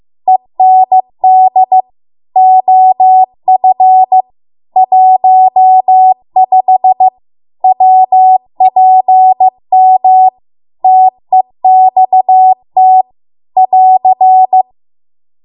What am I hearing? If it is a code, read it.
Morse code, heard as a keyed tone, about 15 wpm, 750 Hz. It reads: END OF 15 WPM TEXT <AR>